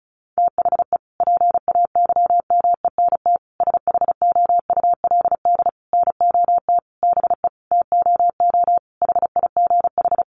Morse code, heard as a keyed tone, about 34 wpm, 700 Hz. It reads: THE PAYMENT SHOULD NOT BE TOO HIGH
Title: Sentences from Top 1200 Words 35wpm